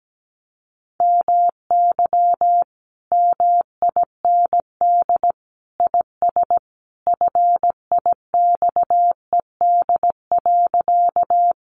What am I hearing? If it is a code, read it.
Morse code, heard as a keyed tone, about 17 wpm, 700 Hz. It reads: MY MIND IS FIXED.
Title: Sentences from Top 1100 Words 17wpm